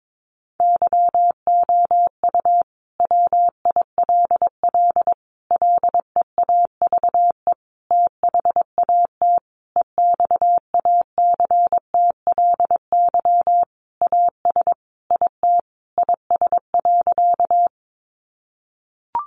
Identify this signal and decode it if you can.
Morse code, heard as a keyed tone, about 22 wpm, 700 Hz. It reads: YOU WILL LEAVE THAT EXACTLY AS IT IS.
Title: Sentences from Top 900 Words 22wpm